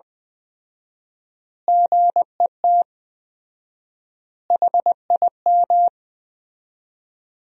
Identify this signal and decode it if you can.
Morse code, heard as a keyed tone, about 21 wpm, 700 Hz. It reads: GET HIM
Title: 4x Word-Spacing - Sentences from Top 400 Words - 20wpm